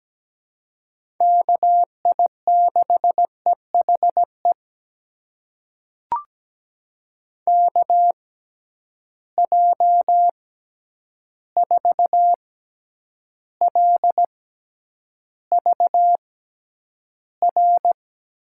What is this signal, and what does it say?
Morse code, heard as a keyed tone, about 17 wpm, spaced as for 9 wpm, 700 Hz. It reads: KI6EHE KJ4LVR
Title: USA Call Signs - 6 Characters - 6x Character Spacing - 17wpm